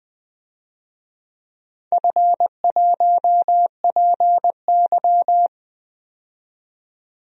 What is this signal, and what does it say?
Morse code, heard as a keyed tone, about 20 wpm, 700 Hz. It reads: F1PY
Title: DXCC-100 Call Signs - 4 Characters - 20wpm